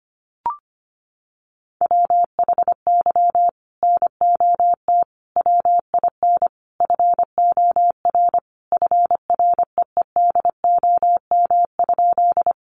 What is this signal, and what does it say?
Morse code, heard as a keyed tone, about 25 wpm, 700 Hz. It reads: WHY NOT WIN FOR FREEDOM?
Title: Sentences from Top 1500 Words 25wpm